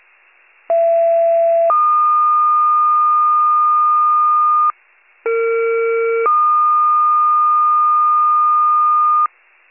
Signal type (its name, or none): two-tone page